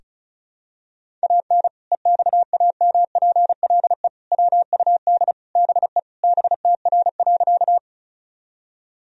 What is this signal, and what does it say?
Morse code, heard as a keyed tone, about 34 wpm, 700 Hz. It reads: AN EXAMPLE WUD BE BTR.
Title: Sentences from Top 400 Words and Abbreviations 35wpm